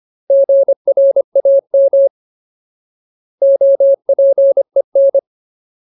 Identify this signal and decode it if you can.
Morse code, heard as a keyed tone, about 25 wpm, 550 Hz. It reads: GRAM OPEN
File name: Ultimate Word List / 25WPM Ultimate